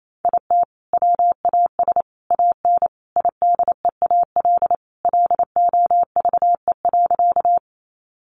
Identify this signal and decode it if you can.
Morse code, heard as a keyed tone, about 28 wpm, 700 Hz. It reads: IT WAS AN IDEAL LOVE.
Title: Sentences from Top 1400 Words 28wpm